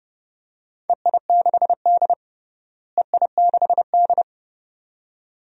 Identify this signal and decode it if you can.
Morse code, heard as a keyed tone, about 30 wpm, 700 Hz. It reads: EI6D EI6D
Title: DXCC-100 Call Signs - 4 Characters - Repeated 2x - 30wpm